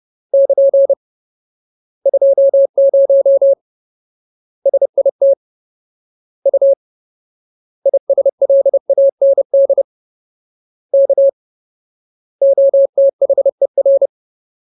Morse code, heard as a keyed tone, about 30 wpm, 550 Hz. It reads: <KN> 20 SIT U ISLAND K OTHER
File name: Ultimate Word List / 30WPM Ultimate